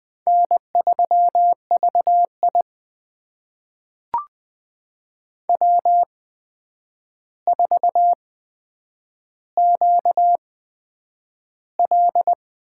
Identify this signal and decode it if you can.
Morse code, heard as a keyed tone, about 20 wpm, spaced as for 9 wpm, 700 Hz. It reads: N3VI W4QL
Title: USA Call Signs - 4 Characters - 8x Character Spacing - 20wpm